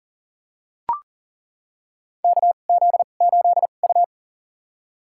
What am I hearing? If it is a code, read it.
Morse code, heard as a keyed tone, about 40 wpm, spaced as for 29 wpm, 700 Hz. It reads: KZ8U
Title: USA Call Signs - 4 Characters - 2x Character Spacing - 40wpm